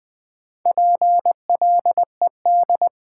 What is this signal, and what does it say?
Morse code, heard as a keyed tone, about 20 wpm, 700 Hz. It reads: PLED
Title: Build A Word - 2 Letters At A Time - 6-Letter Words - 20wpm